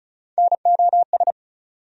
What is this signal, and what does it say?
Morse code, heard as a keyed tone, about 34 wpm, 700 Hz. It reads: NOS
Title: USA Call Sign Postfixes - 3 Letters 35wpm